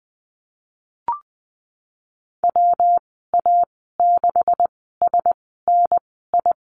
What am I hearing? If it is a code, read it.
Morse code, heard as a keyed tone, about 19 wpm, spaced as for 14 wpm, 700 Hz. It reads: WA6SNI
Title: 128 Lesson 21g - Number 6 - Call Signs 20wpm